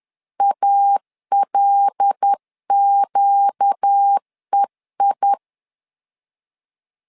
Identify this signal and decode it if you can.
Morse code, heard as a keyed tone, about 10 wpm, 800 Hz. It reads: ALQEI
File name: LEZIONI / 5 LBVC / 5 LBV 10wpm 2248